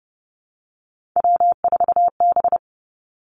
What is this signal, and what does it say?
Morse code, heard as a keyed tone, about 30 wpm, 700 Hz. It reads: W4B
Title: Three Challenging Letter-Number Combinations 30wpm